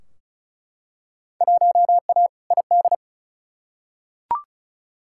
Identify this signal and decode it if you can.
Morse code, heard as a keyed tone, about 34 wpm, 700 Hz. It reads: 1A ID
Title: Field Day Exchanges Encore - 35wpm